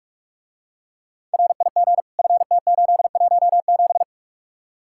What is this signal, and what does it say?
Morse code, heard as a keyed tone, about 48 wpm, 700 Hz. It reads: RIG FT817